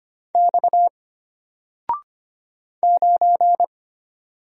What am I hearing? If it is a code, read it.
Morse code, heard as a keyed tone, about 25 wpm, 700 Hz. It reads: X 9
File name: Single-Letter-Number-25wpm